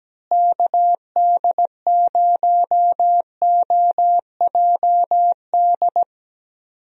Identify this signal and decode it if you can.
Morse code, heard as a keyed tone, about 17 wpm, 700 Hz. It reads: KD0OJD
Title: USA Call Signs - 6 Characters - Repeated 2x - 17wpm